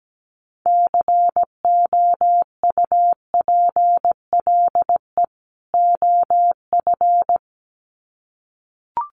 Morse code, heard as a keyed tone, about 17 wpm, 700 Hz. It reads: COUPLE OF
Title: Sets of 2 Words 17wpm